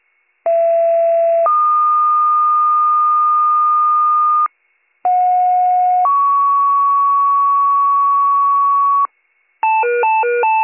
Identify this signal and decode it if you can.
Two-tone page, heard as a one-second tone, then a three-second tone, about 670 then 1220 Hz, about 730 then 1120 Hz. It is heard from 0.4 to 9.1 s.